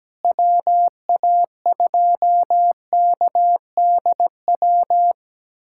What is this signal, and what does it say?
Morse code, heard as a keyed tone, about 17 wpm, 700 Hz. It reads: WA2KDW
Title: USA Callsign DE Callsign - 6 Characters - 17wpm